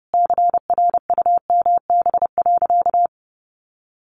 Morse code, heard as a keyed tone, about 30 wpm, 700 Hz. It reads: CRUMB.
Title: Additional Easy Phrases 30wpm